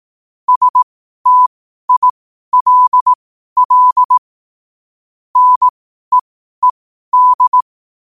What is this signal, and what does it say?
Morse code, heard as a keyed tone, about 18 wpm, spaced as for 12 wpm, 1000 Hz. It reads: STILL NEED
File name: qotdfiles / 12wpm